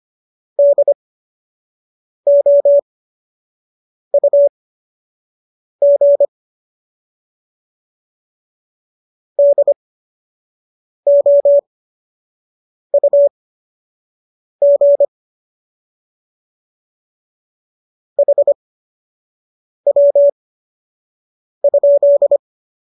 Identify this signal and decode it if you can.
Morse code, heard as a keyed tone, about 25 wpm, spaced as for 6 wpm, 570 Hz. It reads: DOUG DOUG HW?